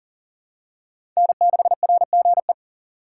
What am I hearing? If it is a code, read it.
Morse code, heard as a keyed tone, about 38 wpm, 700 Hz. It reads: N6RGE
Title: USA Call Signs - 4x Character Spacing - 40wpm